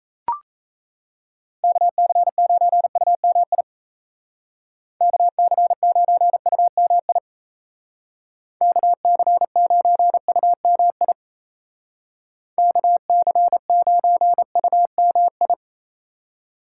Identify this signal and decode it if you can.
Morse code, heard as a keyed tone, about 36 wpm, 700 Hz. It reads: KC9UMI KC9UMI KC9UMI KC9UMI